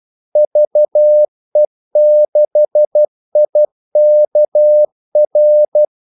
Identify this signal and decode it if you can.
Morse code, heard as a keyed tone, about 12 wpm, 600 Hz. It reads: VE6IKR
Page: 2 ekstra ordmellemrum mellem kaldesignalerne